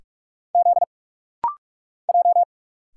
Morse code, heard as a keyed tone, about 44 wpm, 700 Hz. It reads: Z J
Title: Single Letters - Rapid-Fire 45wpm